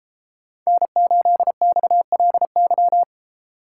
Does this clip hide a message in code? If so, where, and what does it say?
Yes: Morse code from 0.7 to 3.0 s, heard as a keyed tone, about 33 wpm, 700 Hz. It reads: N8XLY